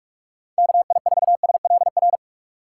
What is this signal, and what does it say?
Morse code, heard as a keyed tone, about 43 wpm, 700 Hz. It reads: KI4SLR
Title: USA Call Signs - 2x Character Spacing - 45wpm